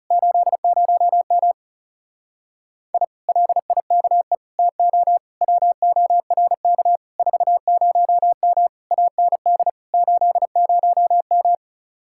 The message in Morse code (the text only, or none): 80M I LIKE TO WORK 40M AND 80M